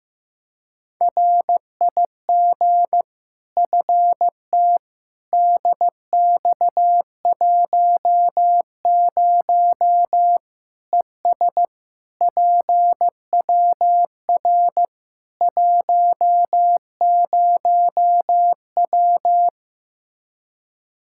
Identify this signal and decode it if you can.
Morse code, heard as a keyed tone, about 16 wpm, 700 Hz. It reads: RIG FT DX10 ES PWR 10W
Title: Practice QSO - Exchange 2 - Info - 15wpm